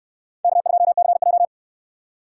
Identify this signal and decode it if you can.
Morse code, heard as a keyed tone, about 64 wpm, 700 Hz. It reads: K2CJ